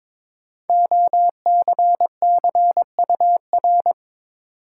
Morse code, heard as a keyed tone, about 22 wpm, 700 Hz. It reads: OCCUR
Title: Sentences from Top 900 Words - New Word Drill - 22wpm